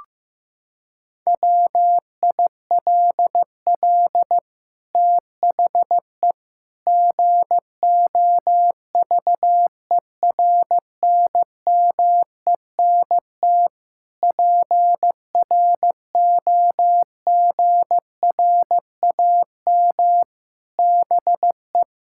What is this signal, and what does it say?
Morse code, heard as a keyed tone, about 16 wpm, 700 Hz. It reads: WILL THE GOVERNMENT PROGRAM BE
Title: Sentences from Top 1000 Words - Review Part 5 - 15wpm